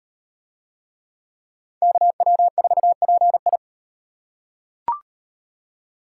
Morse code, heard as a keyed tone, about 35 wpm, 700 Hz. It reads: KW4PI